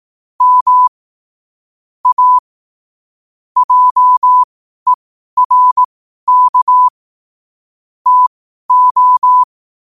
Morse code, heard as a keyed tone, about 18 wpm, spaced as for 12 wpm, 1000 Hz. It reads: M A JERK TO
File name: qotdfiles / 12wpm